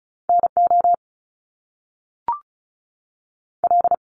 Morse code, heard as a keyed tone, about 36 wpm, 700 Hz. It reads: N O L
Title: Top Two Letter Combinations in Words 35wpm